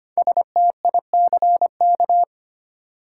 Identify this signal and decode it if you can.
Morse code, heard as a keyed tone, about 25 wpm, 700 Hz. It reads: STICK
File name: 5 Letter Words - First 3 Letters Given - 25wpm